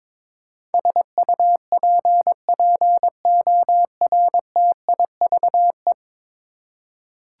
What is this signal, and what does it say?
Morse code, heard as a keyed tone, about 22 wpm, 700 Hz. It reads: SUPPORTIVE
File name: practice / 22wpm / 10 Letter Words - First 3 Letters Given - 22wpm